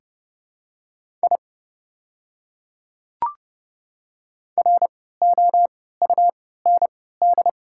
Morse code, heard as a keyed tone, about 30 wpm, spaced as for 17 wpm, 700 Hz. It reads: I ROUND
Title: 3x Character Spacing - Top 200 Words - 30wpm